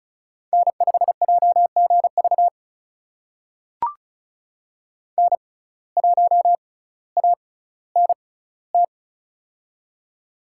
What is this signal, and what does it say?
Morse code, heard as a keyed tone, about 36 wpm, spaced as for 18 wpm, 700 Hz. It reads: N5JGV N1ANT
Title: USA Call Signs - 6x Character Spacing - 35wpm